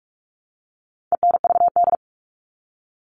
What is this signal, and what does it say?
Morse code, heard as a keyed tone, about 44 wpm, 700 Hz. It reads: ENVD
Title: All Cut Numbers - 4-Digit 45wpm